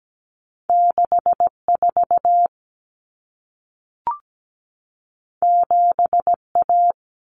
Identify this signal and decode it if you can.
Morse code, heard as a keyed tone, about 17 wpm, 700 Hz. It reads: 64 7A
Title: Common Cut Numbers - 2-Digit 17wpm